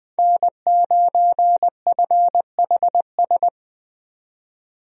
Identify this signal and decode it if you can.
Morse code, heard as a keyed tone, about 20 wpm, 700 Hz. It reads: N9FHS
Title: USA Call Signs - 5 Characters - 20wpm